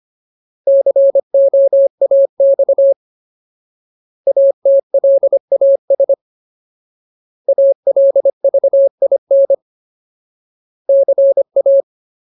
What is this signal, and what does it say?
Morse code, heard as a keyed tone, about 25 wpm, 550 Hz. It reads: COAX ATLAS ALVIN CA
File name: Ultimate Word List / 25WPM Ultimate